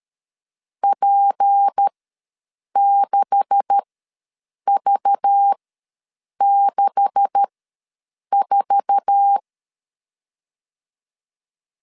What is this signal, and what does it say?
Morse code, heard as a keyed tone, about 13 wpm, spaced as for 7 wpm, 800 Hz. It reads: P6V64
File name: LEZIONI / 9 286 ws 12wpm 3351